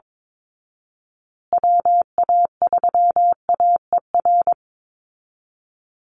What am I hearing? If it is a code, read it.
Morse code, heard as a keyed tone, about 22 wpm, 700 Hz. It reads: WA3AER
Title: Call Signs Repeated 2x - 22wpm